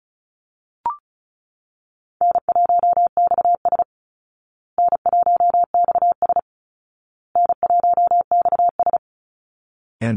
Morse code, heard as a keyed tone, about 34 wpm, 700 Hz. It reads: N1XS N1XS N1XS
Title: Easy Call Signs Repeated 3x - 35wpm